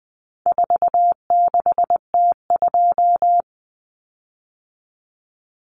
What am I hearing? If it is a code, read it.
Morse code, heard as a keyed tone, about 19 wpm, 700 Hz. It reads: 46T2
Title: Common Cut Numbers - 4-Digit 20wpm